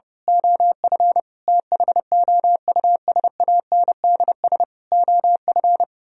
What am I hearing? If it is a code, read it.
Morse code, heard as a keyed tone, about 28 wpm, 700 Hz. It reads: OF THOUSANDS OF
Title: Speed-Racing - Sets of 4 Words 30wpm